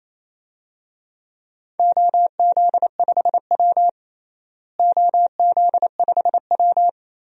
Morse code, heard as a keyed tone, about 28 wpm, 700 Hz. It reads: OZ5W OZ5W
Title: DXCC-100 Call Signs - 4 Characters - Repeated 2x - 28wpm